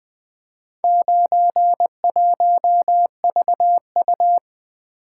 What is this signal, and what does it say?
Morse code, heard as a keyed tone, about 20 wpm, 700 Hz. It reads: 91VU
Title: Four Letter-Number Combinations - 20wpm